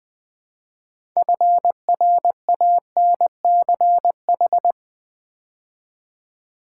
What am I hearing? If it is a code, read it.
Morse code, heard as a keyed tone, about 20 wpm, 700 Hz. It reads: FRANCH
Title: Build A Word - 3 Letters At A Time - 9-Letter Words - 20wpm